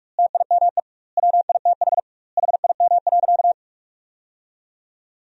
Morse code, heard as a keyed tone, about 46 wpm, 700 Hz. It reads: TIME WITH HIM.
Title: 2x Word-Spacing - Sentences from Top 300 Words - 45wpm